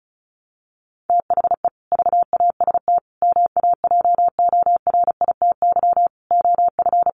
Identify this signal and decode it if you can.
Morse code, heard as a keyed tone, about 34 wpm, 700 Hz. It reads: THE VAST MAJORITY OF
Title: Sets of 5 Words Encore 35wpm